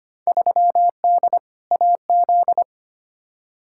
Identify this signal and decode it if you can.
Morse code, heard as a keyed tone, about 25 wpm, 700 Hz. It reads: 3D AZ
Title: Field Day Exchanges - 25wpm